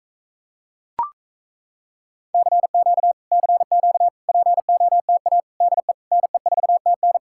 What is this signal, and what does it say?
Morse code, heard as a keyed tone, about 37 wpm, 700 Hz. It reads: CQ CQ POTA DE NE4TN